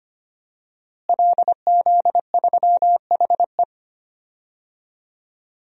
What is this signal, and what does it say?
Morse code, heard as a keyed tone, about 25 wpm, 700 Hz. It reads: LZ3HE
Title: DXCC-100 Call Signs - 5 Characters - Repeated 2x - 25wpm